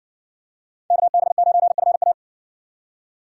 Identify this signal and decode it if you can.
Morse code, heard as a keyed tone, about 60 wpm, 700 Hz. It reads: KB9FA